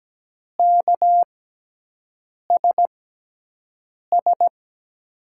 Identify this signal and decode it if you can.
Morse code, heard as a keyed tone, about 17 wpm, spaced as for 6 wpm, 700 Hz. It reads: KSS
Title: USA Call Signs - 6 Characters - 6x Character Spacing - 17wpm